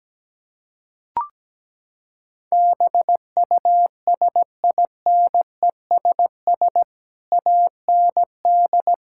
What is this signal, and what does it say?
Morse code, heard as a keyed tone, about 17 wpm, 700 Hz. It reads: BUSINESS AND
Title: Sentences from Top 1000 Words - Review Part 6 - 17wpm